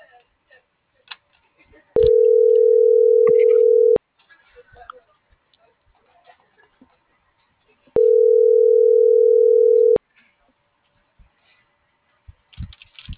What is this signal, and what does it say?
Ringback tone (North America), 440 + 480 Hz.